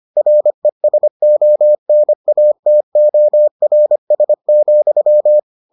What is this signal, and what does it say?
Morse code, heard as a keyed tone, about 25 wpm, 600 Hz. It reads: RESONATORS,